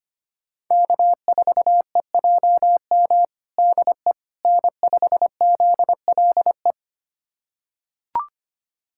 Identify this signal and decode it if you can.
Morse code, heard as a keyed tone, about 25 wpm, 700 Hz. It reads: K4EJM DE N5ZLE
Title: USA Callsign DE Callsign - 5 Characters - 25wpm